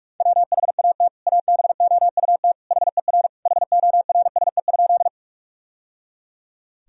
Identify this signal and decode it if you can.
Morse code, heard as a keyed tone, about 46 wpm, 700 Hz. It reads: WHAT ABOUT HER HORSE?
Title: Sentences from Top 1000 Words - Review Part 1 - 45wpm